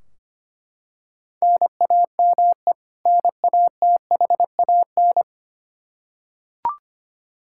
Morse code, heard as a keyed tone, about 25 wpm, 700 Hz. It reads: NAME NATHAN
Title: Practice QSO - Exchange 1 - Beginning - 25wpm